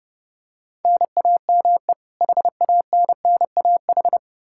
Morse code, heard as a keyed tone, about 32 wpm, 700 Hz. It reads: NAME HANNAH
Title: Practice QSO - Exchange 1 - Beginning - 30wpm